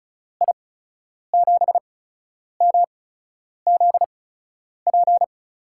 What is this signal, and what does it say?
Morse code, heard as a keyed tone, about 36 wpm, spaced as for 10 wpm, 700 Hz. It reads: I7MZP
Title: USA Call Signs - 8x Character Spacing - 35wpm